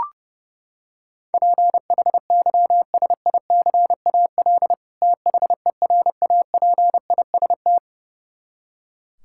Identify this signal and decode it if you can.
Morse code, heard as a keyed tone, about 32 wpm, 700 Hz. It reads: PHYSICAL THERAPIST
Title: Practice QSO - Exchange 3 - More Info - 30wpm